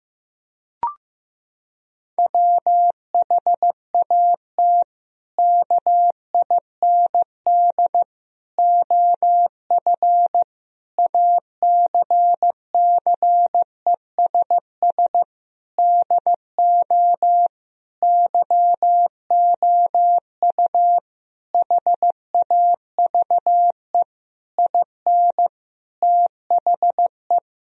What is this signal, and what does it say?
Morse code, heard as a keyed tone, about 15 wpm, 700 Hz. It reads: WHAT KIND OF ACCESS DO YOU HAVE IN THE
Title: Sentences from Top 1000 Words - Review Part 4 - 15wpm